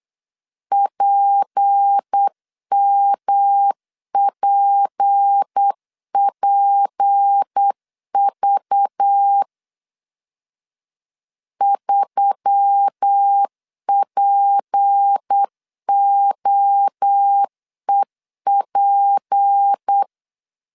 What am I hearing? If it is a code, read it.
Morse code, heard as a keyed tone, about 8 wpm, 800 Hz. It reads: PMPPV 3POEP
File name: LEZIONI / 6 PKX0 / 6 P 8wpm 3025